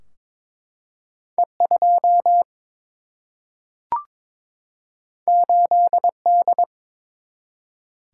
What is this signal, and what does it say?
Morse code, heard as a keyed tone, about 22 wpm, 700 Hz. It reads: E2 8D
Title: Single-Letter Single-Number 22wpm